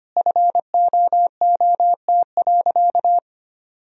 Morse code, heard as a keyed tone, about 25 wpm, 700 Hz. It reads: FOOT.
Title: Sentences from Top 1000 Words - Review Part 3 - 25wpm